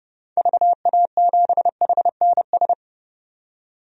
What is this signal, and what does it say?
Morse code, heard as a keyed tone, about 30 wpm, 700 Hz. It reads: VA7HNS
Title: DXCC-100 Call Signs - 6 Characters - 30wpm